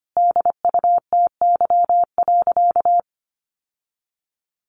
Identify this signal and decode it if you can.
Morse code, heard as a keyed tone, about 25 wpm, 700 Hz. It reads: DUTY.
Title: Sentences from Top 1500 Words 25wpm